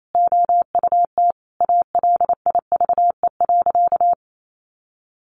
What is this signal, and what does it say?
Morse code, heard as a keyed tone, about 28 wpm, 700 Hz. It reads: OUT ALIVE.
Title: A Sound Of Thunder 28wpm